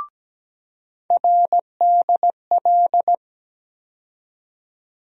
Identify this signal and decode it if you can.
Morse code, heard as a keyed tone, about 17 wpm, 700 Hz. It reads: RDL
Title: USA Call Sign Postfixes - 3 Letters 17wpm